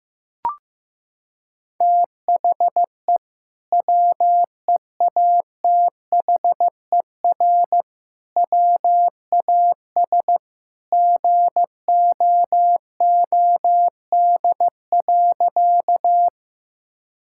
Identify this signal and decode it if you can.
Morse code, heard as a keyed tone, about 15 wpm, 700 Hz. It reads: THE WEATHER WAS GOOD.
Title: Sentences from Top 1000 Words - Review Part 5 - 15wpm